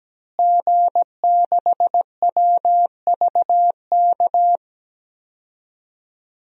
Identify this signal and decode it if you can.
Morse code, heard as a keyed tone, about 17 wpm, 700 Hz. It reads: G6WVK
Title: DXCC-100 Call Signs - 5 Characters - Repeated 3x - 17wpm